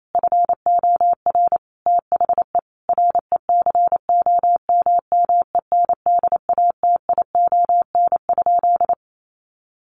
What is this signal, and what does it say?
Morse code, heard as a keyed tone, about 28 wpm, 700 Hz. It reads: FOR THE RECOMMENDATION?
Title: Sentences from Top 1400 Words 28wpm